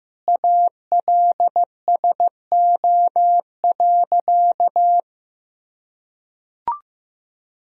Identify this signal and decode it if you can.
Morse code, heard as a keyed tone, about 15 wpm, 700 Hz. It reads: ALSO.
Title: Sentences from Top 1000 Words - Review Part 1 - 15wpm